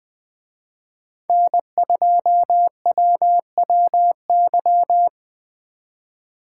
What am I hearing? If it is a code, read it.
Morse code, heard as a keyed tone, about 20 wpm, 700 Hz. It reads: N2WWY
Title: Speed Racing - USA 1x3 Call Signs 17wpm